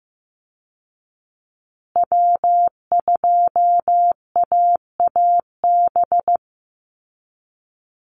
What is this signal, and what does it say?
Morse code, heard as a keyed tone, about 15 wpm, 700 Hz. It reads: W2AAB
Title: Call Signs Repeated 2x - 15wpm